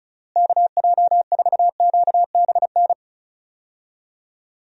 Morse code, heard as a keyed tone, about 34 wpm, 700 Hz. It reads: KJ4QBN
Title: USA Call Signs - 6 Characters - Repeated 2x - 35wpm